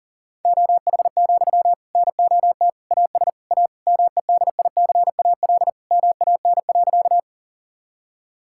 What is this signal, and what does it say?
Morse code, heard as a keyed tone, about 40 wpm, 700 Hz. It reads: OH, NOT AS A MEDICAL MAN.
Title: Sentences from Top 1000 Words - Review Part 5 - 40wpm